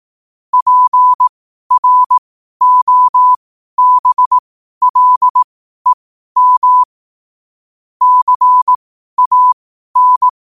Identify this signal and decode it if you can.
Morse code, heard as a keyed tone, about 18 wpm, spaced as for 12 wpm, 1000 Hz. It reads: PROBLEM CAN
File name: qotdfiles / 12wpm